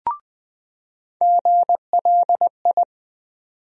Morse code, heard as a keyed tone, about 20 wpm, 700 Hz. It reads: GLI
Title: USA Call Sign Postfixes - 3 Letters 20wpm